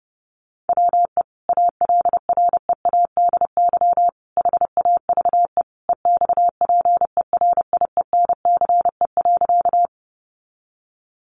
Morse code, heard as a keyed tone, about 30 wpm, 700 Hz. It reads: WE ALREADY HAVE EXPERIENCE.
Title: Sentences from Top 700 Words 30wpm